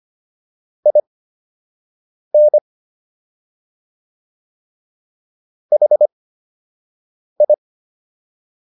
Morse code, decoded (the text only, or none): IN HI